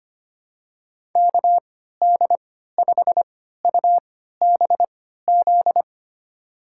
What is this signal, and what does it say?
Morse code, heard as a keyed tone, about 25 wpm, spaced as for 14 wpm, 700 Hz. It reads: KD5UBZ
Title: USA Call Signs - 6 Characters - 3x Character Spacing - 25wpm